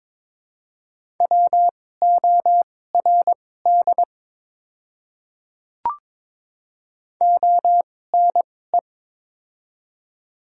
Morse code, heard as a keyed tone, about 22 wpm, spaced as for 16 wpm, 700 Hz. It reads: WORD ONE
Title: 2x Character Spacing - Top 500 Words - 22wpm